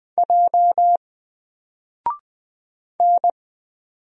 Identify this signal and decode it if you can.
Morse code, heard as a keyed tone, about 20 wpm, 700 Hz. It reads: J N
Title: Single Character (Letter:Number:Punctuation:Prosign) 20wpm